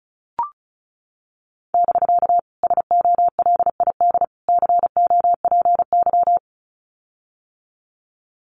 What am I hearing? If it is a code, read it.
Morse code, heard as a keyed tone, about 34 wpm, 700 Hz. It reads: <BK> SOLID COPY
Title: Common QSO Phrases Encore 35wpm